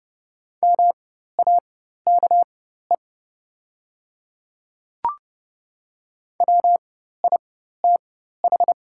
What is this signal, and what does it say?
Morse code, heard as a keyed tone, about 30 wpm, spaced as for 14 wpm, 690 Hz. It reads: MAKE WITH
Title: 4x Character Spacing - Top 100 Words - 30wpm